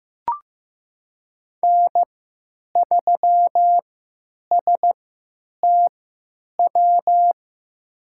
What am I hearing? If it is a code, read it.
Morse code, heard as a keyed tone, about 16 wpm, spaced as for 9 wpm, 700 Hz. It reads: N3STW
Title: USA Call Signs - 5 Characters - 3x Character Spacing - 15wpm